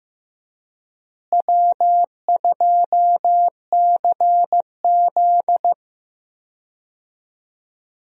Morse code, heard as a keyed tone, about 15 wpm, 700 Hz. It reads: W2CZ
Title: USA Call Signs - 4 Characters - Repeated 2x - 15wpm